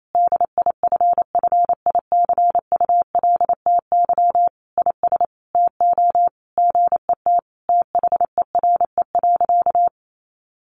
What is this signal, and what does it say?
Morse code, heard as a keyed tone, about 28 wpm, 700 Hz. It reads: DIFFICULTY IS TO GET THERE.
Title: Sentences from Top 1500 Words 28wpm